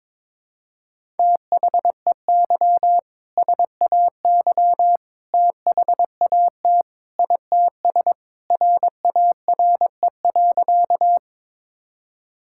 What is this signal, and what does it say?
Morse code, heard as a keyed tone, about 22 wpm, 700 Hz. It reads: THEY SAY THAT ITS RARE.
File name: practice / 22wpm / Sentences with 2-Word Phrases - 22wpm